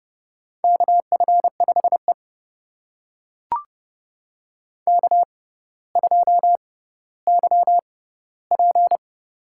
Morse code, heard as a keyed tone, about 30 wpm, spaced as for 15 wpm, 700 Hz. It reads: KF5E K2YP
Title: USA Call Signs - 4 Characters - 6x Character Spacing - 30wpm